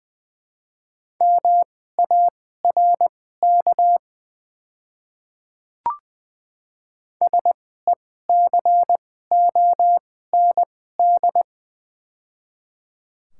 Morse code, heard as a keyed tone, about 20 wpm, spaced as for 14 wpm, 700 Hz. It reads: MARK SECOND
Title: 2x Character Spacing - Top 500 Words - 20wpm